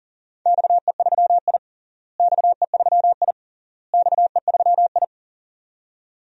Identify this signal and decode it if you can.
Morse code, heard as a keyed tone, about 40 wpm, 700 Hz. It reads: XE3I XE3I XE3I
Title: DXCC-100 Call Signs - 4 Characters - Repeated 3x - 40wpm